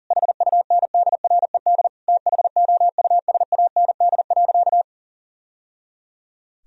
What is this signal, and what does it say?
Morse code, heard as a keyed tone, about 40 wpm, 700 Hz. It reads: HUNDRED THOUSAND.
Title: Sentences from Top 1000 Words - Review Part 3 - 40wpm